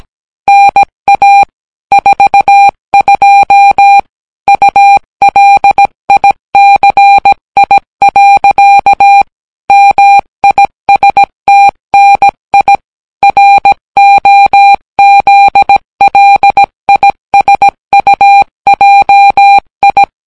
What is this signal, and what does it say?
Morse code, heard as a keyed tone, about 17 wpm, 800 Hz. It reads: NA 42 ULICI. MISTNI ROZLISUJI